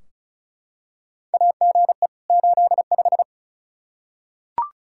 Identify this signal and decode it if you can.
Morse code, heard as a keyed tone, about 36 wpm, 700 Hz. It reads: AGE 85
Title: Practice QSO - Exchange 3 - More Info - 35wpm